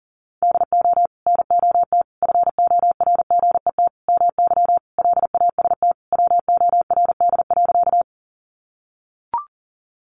Morse code, heard as a keyed tone, about 40 wpm, 700 Hz. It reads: DO NOT FORGET MY LAST WORD.
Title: Sentences from Top 1200 Words 40wpm